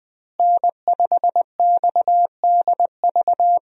Morse code, heard as a keyed tone, about 20 wpm, 700 Hz. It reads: N5XDV
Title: USA Call Signs - 5 Characters - Repeated 2x - 20wpm